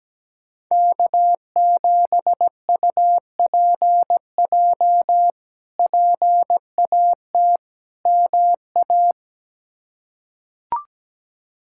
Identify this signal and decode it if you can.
Morse code, heard as a keyed tone, about 17 wpm, 700 Hz. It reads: K7UPJ PAT MA
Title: K1USN SST QSO Practice - 17wpm